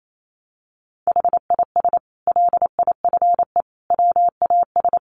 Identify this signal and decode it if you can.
Morse code, heard as a keyed tone, about 28 wpm, 700 Hz. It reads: HIS LIFE WAS
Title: Sentences from Top 1100 Words 28wpm